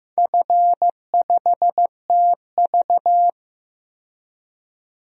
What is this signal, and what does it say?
Morse code, heard as a keyed tone, about 15 wpm, 700 Hz. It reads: F5TV
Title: DXCC-100 Call Signs - 4 Characters - 15wpm